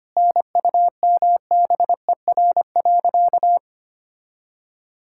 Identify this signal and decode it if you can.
Morse code, heard as a keyed tone, about 25 wpm, 700 Hz. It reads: NUMBER.
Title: Sentences from Top 1000 Words - Review Part 6 - 25wpm